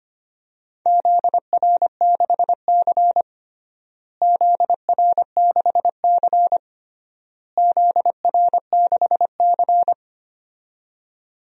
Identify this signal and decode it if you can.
Morse code, heard as a keyed tone, about 25 wpm, 700 Hz. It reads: ZR6C ZR6C ZR6C
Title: DXCC-100 Call Signs - 4 Characters - Repeated 3x - 25wpm